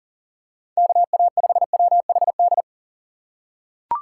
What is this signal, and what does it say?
Morse code, heard as a keyed tone, about 40 wpm, 700 Hz. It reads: KA5WHD
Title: USA Call Signs - 6 Characters - 4x Character Spacing - 40wpm